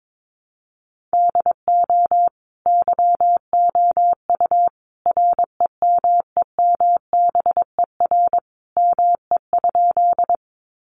Morse code, heard as a keyed tone, about 22 wpm, 700 Hz. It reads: DO YOU REMEMBER ME?
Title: Sentences from Top 400 Words 22wpm